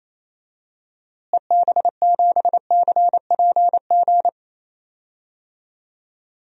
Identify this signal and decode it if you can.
Morse code, heard as a keyed tone, about 27 wpm, 700 Hz. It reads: EB7CPG
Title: DXCC-100 Call Signs - 6 Characters - Repeated 2x - 28wpm